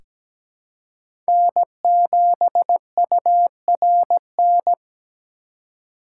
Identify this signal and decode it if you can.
Morse code, heard as a keyed tone, about 17 wpm, 700 Hz. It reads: N7URN
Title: USA Call Signs - 5 Characters - 6x Character Spacing - 17wpm